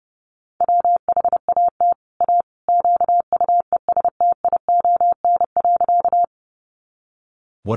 Morse code, heard as a keyed tone, about 30 wpm, 700 Hz. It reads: WHAT A QUESTION.
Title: Sentences from Top 400 Words 30wpm